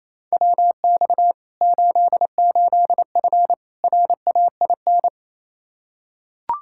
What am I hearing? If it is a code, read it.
Morse code, heard as a keyed tone, about 28 wpm, 700 Hz. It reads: WX 88F RAIN
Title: Practice QSO - Exchange 2 - Info - 28wpm